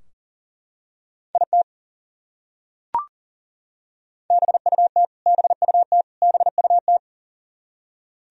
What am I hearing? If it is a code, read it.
Morse code, heard as a keyed tone, about 40 wpm, 700 Hz. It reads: IT BUT BUT BUT
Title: Top 100 Words Repeated 40wpm